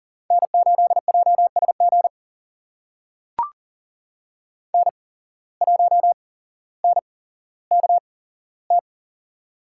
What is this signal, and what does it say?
Morse code, heard as a keyed tone, about 39 wpm, spaced as for 17 wpm, 700 Hz. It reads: N8JSG N1NKT
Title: USA Call Signs - 5 Characters - 8x Character Spacing - 40wpm